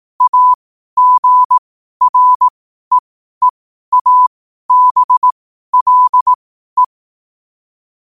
Morse code, heard as a keyed tone, about 18 wpm, spaced as for 12 wpm, 1000 Hz. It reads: AGREEABLE
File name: qotdfiles / 12wpm